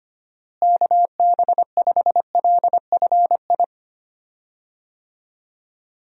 Morse code, heard as a keyed tone, about 25 wpm, 700 Hz. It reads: KB5LFI
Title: USA Call Signs - 6 Characters - Repeated 3x - 25wpm